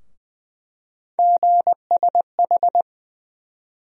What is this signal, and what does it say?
Morse code, heard as a keyed tone, about 20 wpm, 700 Hz. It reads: GSH
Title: USA Call Sign Postfixes - 3 Letters 20wpm